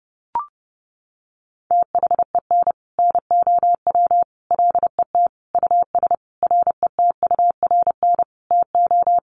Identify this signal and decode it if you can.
Morse code, heard as a keyed tone, about 30 wpm, 700 Hz. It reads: THEN NOW LET US RETURN TO
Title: Sentences from Top 1500 Words 30wpm